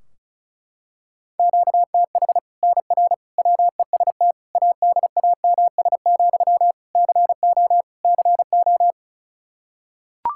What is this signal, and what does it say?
Morse code, heard as a keyed tone, about 36 wpm, 700 Hz. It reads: QTH NR WEST ADAMS, CO CO
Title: Practice QSO - Exchange 1 - Beginning - 35wpm